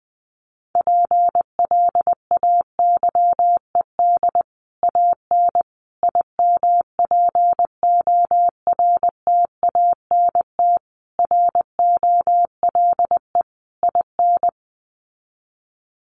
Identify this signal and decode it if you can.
Morse code, heard as a keyed tone, about 20 wpm, 700 Hz. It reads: PLAYED AN IMPORTANT ROLE IN
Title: Sets of 5 Words Encore2 20wpm